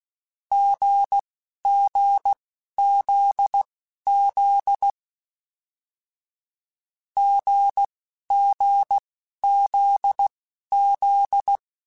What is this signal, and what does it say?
Morse code, heard as a keyed tone, about 16 wpm, spaced as for 11 wpm, 800 Hz. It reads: GGZZ GGZZ
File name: Short Runs / Short Runs-mp3s / GZ GZ7 @10